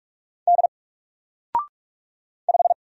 Morse code, heard as a keyed tone, about 44 wpm, 700 Hz. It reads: D 5
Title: Single Letter-Number 45wpm